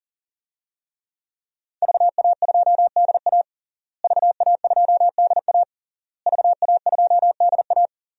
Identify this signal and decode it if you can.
Morse code, heard as a keyed tone, about 40 wpm, 700 Hz. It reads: VA2DA VA2DA VA2DA
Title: DXCC-100 Call Signs - 5 Characters - Repeated 3x - 40wpm